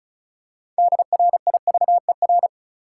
Morse code, heard as a keyed tone, about 36 wpm, 700 Hz. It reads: DRIVER
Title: Six Letter Words - 35wpm